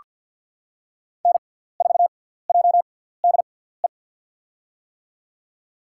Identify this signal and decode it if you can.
Morse code, heard as a keyed tone, about 54 wpm, spaced as for 17 wpm, 700 Hz. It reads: N4JDE
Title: USA Call Signs - 5 Characters - 6x Character Spacing - 50wpm